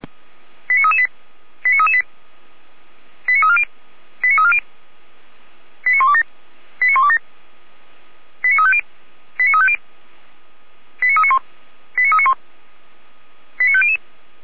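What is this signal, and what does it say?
Five-tone selective call (ZVEI), heard as quick runs of five tones, tones of about 70 ms. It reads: 89338 89338 89340 89340 89127 89127 89350 89350 89391 89391 89500
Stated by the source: ZVEI five-tone selective call received by SDR.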